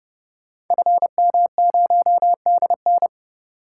Morse code, heard as a keyed tone, about 30 wpm, 690 Hz. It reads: FM0DN
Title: Code Groups - 5 Letter:Numbers - 30wpm